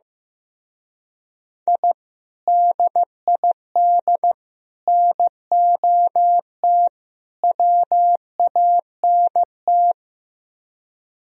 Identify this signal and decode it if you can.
Morse code, heard as a keyed tone, about 15 wpm, 700 Hz. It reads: I DID NOT WANT
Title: Speed-Racing - Sets of 4 Words 15wpm